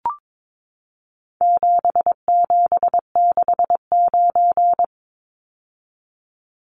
Morse code, heard as a keyed tone, about 22 wpm, 700 Hz. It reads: 7769
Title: Numbers - 4-Digit 22wpm